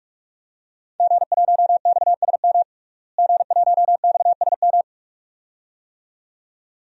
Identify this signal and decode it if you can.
Morse code, heard as a keyed tone, about 46 wpm, 700 Hz. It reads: G1XSM G1XSM
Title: DXCC-100 Call Signs - 5 Characters - Repeated 2x - 45wpm